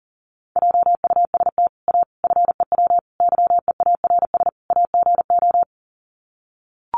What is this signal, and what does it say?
Morse code, heard as a keyed tone, about 40 wpm, 700 Hz. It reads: JUST A FEW YEARS AGO
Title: Sets of 5 Words Encore 40wpm